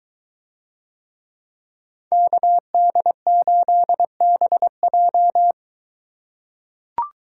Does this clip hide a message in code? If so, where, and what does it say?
Yes: Morse code from 2.1 to 5.5 s, heard as a keyed tone, about 21 wpm, 700 Hz. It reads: KD8BJ